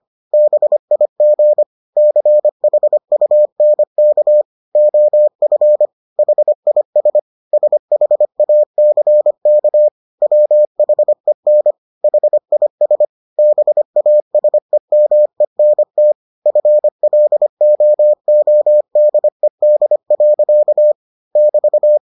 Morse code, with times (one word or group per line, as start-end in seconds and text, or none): 0.3-1.6 BIG
2.0-4.4 CHUNK
4.8-5.9 OF
6.2-7.2 HIS
7.5-9.9 SHACK
10.2-11.7 WHEN
12.0-13.0 HIS
13.4-16.1 BASEMENT
16.5-20.9 FLOODED.
21.4-22.0 <BT>